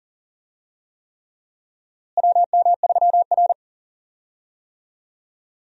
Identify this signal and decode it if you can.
Morse code, heard as a keyed tone, about 40 wpm, 700 Hz. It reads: WM3R
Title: USA Call Signs - 4 Characters - Repeated 3x - 40wpm